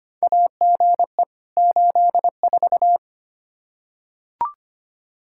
Morse code, heard as a keyed tone, about 25 wpm, 700 Hz. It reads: AGE 84
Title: Practice QSO - Exchange 3 - More Info - 25wpm